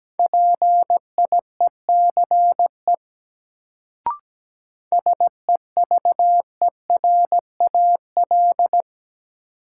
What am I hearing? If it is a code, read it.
Morse code, heard as a keyed tone, about 17 wpm, 700 Hz. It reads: PIECE SEVERAL
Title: Sentences from Top 400 Words - New Word Drill - 17wpm